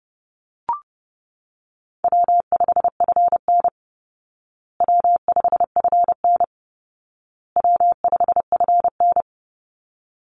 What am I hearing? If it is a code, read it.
Morse code, heard as a keyed tone, about 30 wpm, 700 Hz. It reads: W5FN W5FN W5FN
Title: Call Signs Repeated 2x - 30wpm